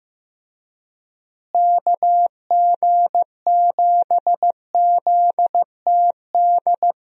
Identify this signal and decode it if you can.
Morse code, heard as a keyed tone, about 15 wpm, 700 Hz. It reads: KG7ZTD
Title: USA Call Signs - 6 Characters - Repeated 3x - 15wpm